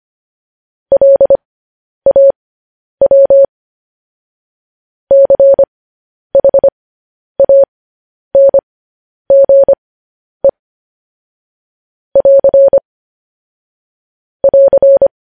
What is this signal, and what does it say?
Morse code, heard as a keyed tone, about 25 wpm, spaced as for 10 wpm, 550 Hz. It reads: LAW CHANGE <AR> <AR>
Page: news headlines in morse code at 10 wpm